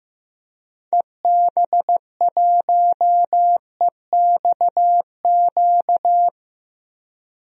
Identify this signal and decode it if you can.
Morse code, heard as a keyed tone, about 15 wpm, 700 Hz. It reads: EB1EXQ
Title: DXCC-100 Call Signs - 6 Characters - 15wpm